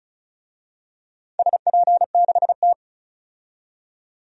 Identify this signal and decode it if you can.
Morse code, heard as a keyed tone, about 36 wpm, 700 Hz. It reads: SP6T
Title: DXCC-100 Call Signs - 4 Characters - 35wpm